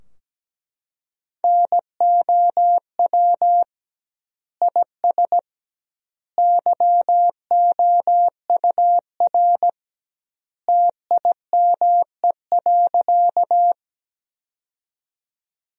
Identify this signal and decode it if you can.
Morse code, heard as a keyed tone, about 17 wpm, 700 Hz. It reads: NOW IS YOUR TIME.
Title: 2x Word-Spacing - Sentences from Top 100 Words - 17wpm